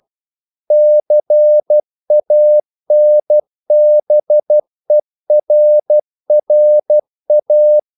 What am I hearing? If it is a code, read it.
Morse code, heard as a keyed tone, about 12 wpm, 600 Hz. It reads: CANBERRA
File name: CW / Practice / NationalCapitals12